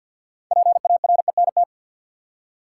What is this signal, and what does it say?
Morse code, heard as a keyed tone, about 50 wpm, 700 Hz. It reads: PARENT
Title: Six Letter Words - 50wpm